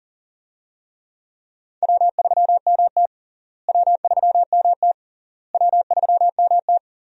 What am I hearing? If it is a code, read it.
Morse code, heard as a keyed tone, about 40 wpm, 700 Hz. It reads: W3MT W3MT W3MT
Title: USA Call Signs - 4 Characters - Repeated 3x - 40wpm